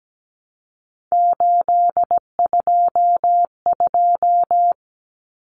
Morse code, heard as a keyed tone, about 17 wpm, 700 Hz. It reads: 822
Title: Common Cut Numbers - 3-Digit 17wpm